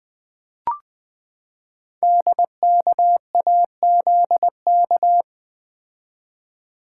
Morse code, heard as a keyed tone, about 21 wpm, 700 Hz. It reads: DKAZK
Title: Code Groups - 5 Letters - 20wpm